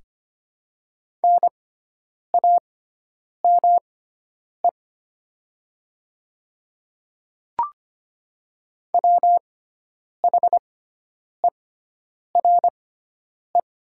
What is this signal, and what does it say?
Morse code, heard as a keyed tone, about 25 wpm, spaced as for 9 wpm, 700 Hz. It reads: NAME WHERE
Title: 6x Character Spacing - Top 200 Words - 25wpm